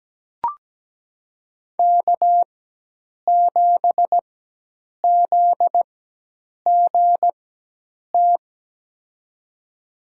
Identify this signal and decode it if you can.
Morse code, heard as a keyed tone, about 17 wpm, spaced as for 8 wpm, 700 Hz. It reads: K7ZGT
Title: USA Call Signs - 5 Characters - 4x Character Spacing - 17wpm